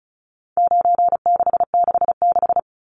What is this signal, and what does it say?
Morse code, heard as a keyed tone, about 34 wpm, 700 Hz. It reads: 9666
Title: Numbers - 4-Digit 35wpm